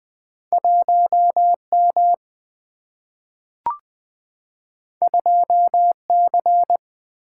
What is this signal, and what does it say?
Morse code, heard as a keyed tone, about 20 wpm, 700 Hz. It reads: 1 M 2 C
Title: Single-Letter Single-Number 20wpm